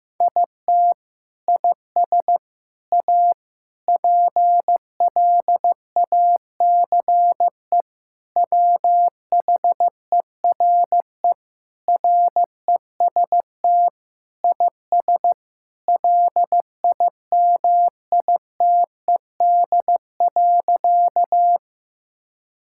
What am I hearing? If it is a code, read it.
Morse code, heard as a keyed tone, about 16 wpm, 700 Hz. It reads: IT IS A PLACE WHERE REST IS LIMITED.
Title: Sentences from Top 1000 Words - Review Part 6 - 15wpm